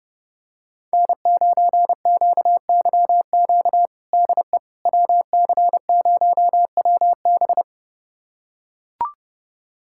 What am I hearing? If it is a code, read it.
Morse code, heard as a keyed tone, about 31 wpm, 700 Hz. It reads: N9QYQ DE WC0WB
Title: USA Callsign DE Callsign - 5 Characters - 30wpm